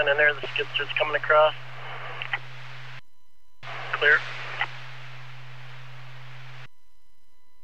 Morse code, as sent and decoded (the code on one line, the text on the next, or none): none